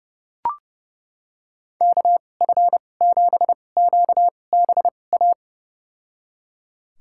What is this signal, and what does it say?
Morse code, heard as a keyed tone, about 30 wpm, spaced as for 22 wpm, 700 Hz. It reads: KF7QBA
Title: USA Call Signs - 6 Characters - 2x Character Spacing - 30wpm